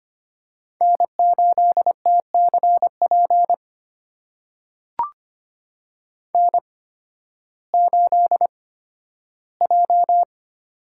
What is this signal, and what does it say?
Morse code, heard as a keyed tone, about 25 wpm, spaced as for 13 wpm, 700 Hz. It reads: N8TCP N8J
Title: USA Call Signs - 5 Characters - 8x Character Spacing - 25wpm